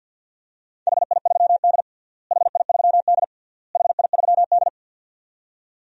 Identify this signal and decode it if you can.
Morse code, heard as a keyed tone, about 53 wpm, 700 Hz. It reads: HI3D HI3D HI3D
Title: DXCC-100 Call Signs - 4 Characters - Repeated 3x - 50wpm